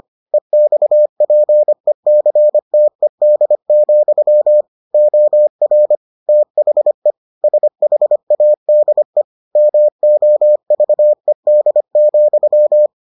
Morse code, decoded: EXPECTED, OR THE SHADE MOVED,

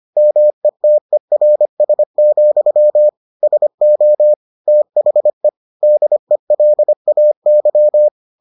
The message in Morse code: METERS, SO THE DELAY

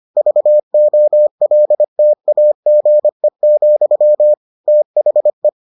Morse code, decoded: VOLTAGE, THE

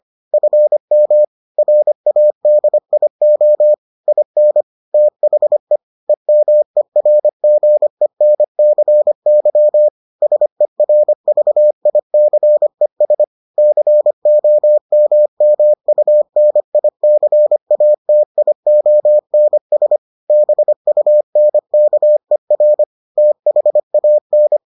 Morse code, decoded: FM RADIO IN THE EMERGENCY SERVICES COMMUNICATIONS BUNKER THAN

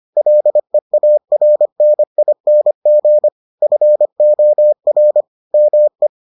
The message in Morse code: LEARNING FOR ME

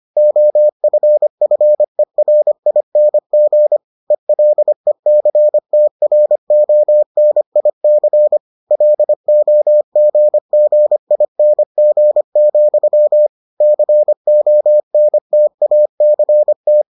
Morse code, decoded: OFFERING ELECTRONIC LOGGING, CONTACT

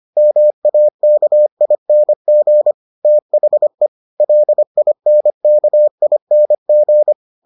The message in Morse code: MAKING THE LINKING

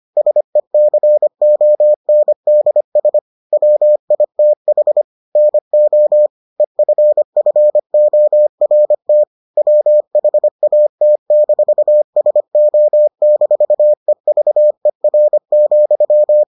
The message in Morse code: SECONDS WITH NO EFFORT WHAT-SO-EVER,

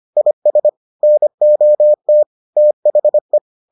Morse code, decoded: IS NOT THE

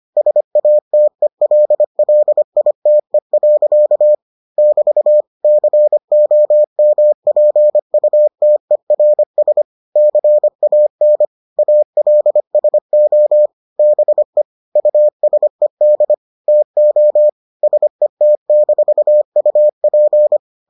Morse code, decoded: SATELLITE. <BT> COMPUTERS CAN ALSO BE USED TO SET-UP

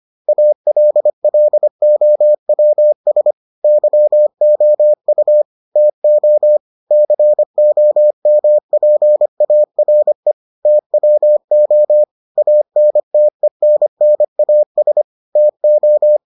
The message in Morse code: ALLOWS YOU TO COMPARE TWO ANTENNAS TO

